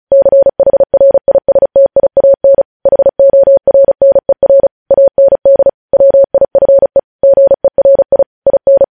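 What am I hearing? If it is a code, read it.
Morse code, heard as a keyed tone, about 36 wpm, 550 Hz. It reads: CHRISTIAN HORNER AND WIFE GERI IN
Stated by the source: Morse practice at 35 wpm sending news headlines.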